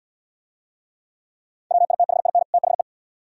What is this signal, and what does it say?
Morse code, heard as a keyed tone, about 78 wpm, 700 Hz. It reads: WITHIN THE